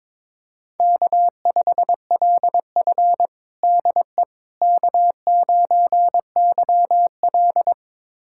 Morse code, heard as a keyed tone, about 21 wpm, 700 Hz. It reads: K5LF DE K9YL
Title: USA Callsign DE Callsign - 4 Characters - 22wpm